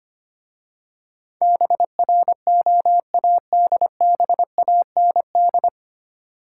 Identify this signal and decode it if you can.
Morse code, heard as a keyed tone, about 25 wpm, 700 Hz. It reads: BROADBAND
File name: practice / 25wpm / 9 Letter Words - First 3 Letters Given - 25wpm